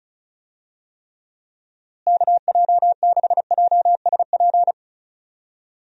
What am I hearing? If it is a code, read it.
Morse code, heard as a keyed tone, about 34 wpm, 700 Hz. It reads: KJ6JSP
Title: USA Call Signs - 6 Characters - Repeated 2x - 35wpm